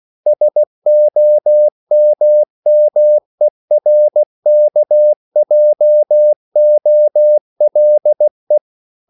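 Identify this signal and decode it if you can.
Morse code, heard as a keyed tone, about 16 wpm, 600 Hz. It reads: SOMMERKJOLE
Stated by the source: Slow (QRS) Morse code at 16 wpm, from CW med Gnister.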